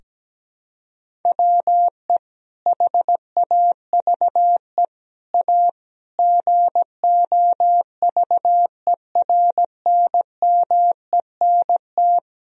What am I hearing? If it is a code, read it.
Morse code, heard as a keyed tone, about 17 wpm, 700 Hz. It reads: WE HAVE A GOVERNMENT
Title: Sentences from Top 1000 Words - Review Part 1 - 17wpm